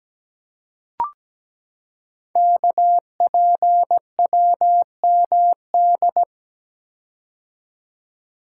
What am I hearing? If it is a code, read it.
Morse code, heard as a keyed tone, about 17 wpm, 700 Hz. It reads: KPWMD
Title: Code Groups - 5 Letters - 17wpm